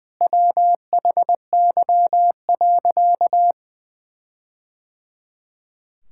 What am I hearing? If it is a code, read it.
Morse code, heard as a keyed tone, about 20 wpm, 700 Hz. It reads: WHY.
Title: Sentences from Top 1600 Words 20wpm